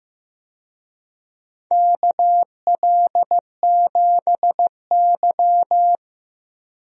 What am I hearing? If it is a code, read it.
Morse code, heard as a keyed tone, about 15 wpm, 700 Hz. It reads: KL7Y
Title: DXCC-100 Call Signs - 4 Characters - 15wpm